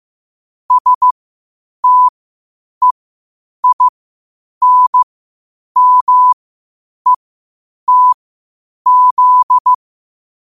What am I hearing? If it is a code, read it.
Morse code, heard as a keyed tone, about 15 wpm, spaced as for 8 wpm, 1000 Hz. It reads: STEINMETZ
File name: qotdfiles / 8wpm